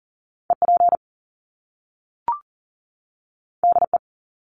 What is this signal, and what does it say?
Morse code, heard as a keyed tone, about 40 wpm, 700 Hz. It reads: EP DE
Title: Top Two Letter Combinations in Words 40wpm